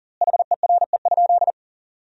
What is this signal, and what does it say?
Morse code, heard as a keyed tone, about 37 wpm, 700 Hz. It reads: HERE?
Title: 4x Word-Spacing - Sentences from Top 200 Words - 40wpm